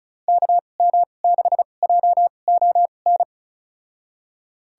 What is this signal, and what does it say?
Morse code, heard as a keyed tone, about 34 wpm, spaced as for 25 wpm, 700 Hz. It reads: KM6JON
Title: USA Call Signs - 6 Characters - 2x Character Spacing - 35wpm